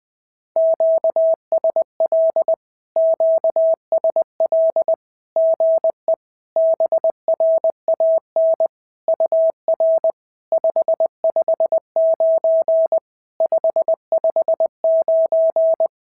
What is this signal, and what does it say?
Morse code, heard as a keyed tone, about 19 wpm, 650 Hz. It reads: QSL QSL GE BRAN UR 559 559
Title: SOTA QSO Practice 20wpm